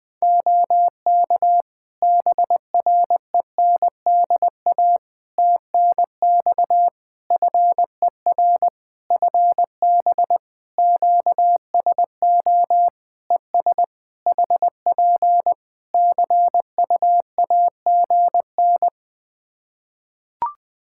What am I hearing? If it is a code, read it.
Morse code, heard as a keyed tone, about 21 wpm, 700 Hz. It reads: OK BRENDA TNX FER FB QSO ES HP CUAGN
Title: Practice QSO - Exchange 4 - Ending - 20wpm